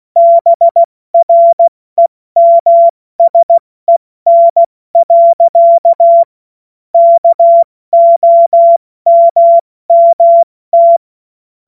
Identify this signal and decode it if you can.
Morse code, heard as a keyed tone, about 16 wpm, 680 Hz. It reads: BREMSEN. KOMMT
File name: Morsekurs / Witze / Witz 14. Dezember 2021, 70 F80